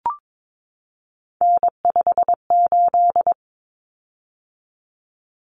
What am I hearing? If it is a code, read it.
Morse code, heard as a keyed tone, about 22 wpm, 700 Hz. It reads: N58
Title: Common Cut Numbers - 3-Digit 22wpm